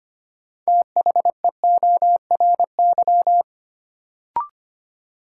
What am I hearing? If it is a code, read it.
Morse code, heard as a keyed tone, about 25 wpm, 700 Hz. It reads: THEORY
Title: Sentences from Top 600 Words - New Word Drill - 25wpm